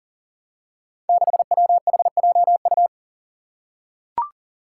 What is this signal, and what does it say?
Morse code, heard as a keyed tone, about 39 wpm, 700 Hz. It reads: 6WHJU
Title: Five Challenging Letter-Number Combinations 40wpm